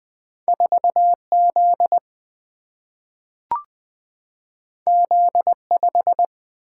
Morse code, heard as a keyed tone, about 20 wpm, 700 Hz. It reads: 4 Z Z 5
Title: Single-Letter Single-Number 20wpm